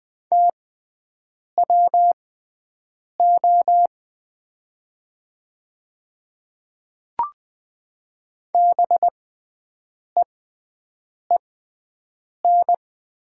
Morse code, heard as a keyed tone, about 20 wpm, spaced as for 7 wpm, 700 Hz. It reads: TWO BEEN
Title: 6x Character Spacing - Top 300 Words - 20wpm